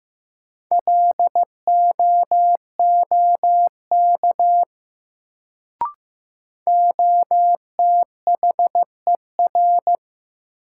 Morse code, heard as a keyed tone, about 15 wpm, 700 Hz. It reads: LOOK OTHER
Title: Sentences from Top 100 Words - New Word Drill - 15wpm